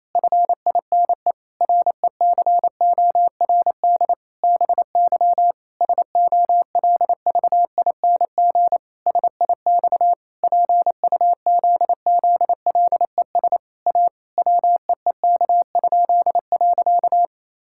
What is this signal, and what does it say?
Morse code, heard as a keyed tone, about 28 wpm, 700 Hz. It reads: FINE RECORD BY SOLVING SIX PUZZLES A WEEK?.
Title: Pangram Challenge v2 - 28wpm